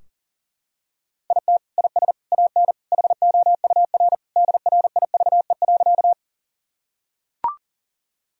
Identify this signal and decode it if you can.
Morse code, heard as a keyed tone, about 40 wpm, 700 Hz. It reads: IT IS AN HOUR DRIVE.
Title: Sentences from Top 1000 Words - Review Part 2 - 40wpm